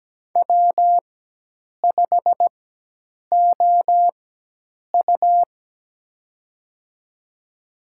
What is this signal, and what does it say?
Morse code, heard as a keyed tone, about 17 wpm, spaced as for 8 wpm, 700 Hz. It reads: W5OU
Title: USA Call Signs - 4x Character Spacing - 17wpm